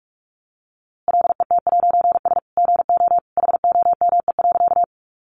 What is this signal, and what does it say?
Morse code, heard as a keyed tone, about 44 wpm, 700 Hz. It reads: LET'S GO HOME.
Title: A Sound Of Thunder 45wpm